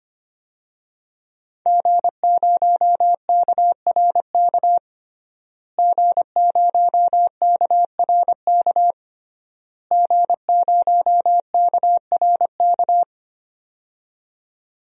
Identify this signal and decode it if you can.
Morse code, heard as a keyed tone, about 25 wpm, 700 Hz. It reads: G0KRK G0KRK G0KRK
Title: DXCC-100 Call Signs - 5 Characters - Repeated 3x - 25wpm